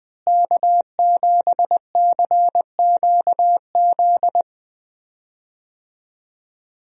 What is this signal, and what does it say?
Morse code, heard as a keyed tone, about 20 wpm, 700 Hz. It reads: K7CQZ
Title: USA Call Signs - 5 Characters - Repeated 3x - 20wpm